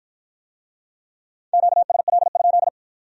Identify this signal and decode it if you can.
Morse code, heard as a keyed tone, about 48 wpm, 700 Hz. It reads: QSL?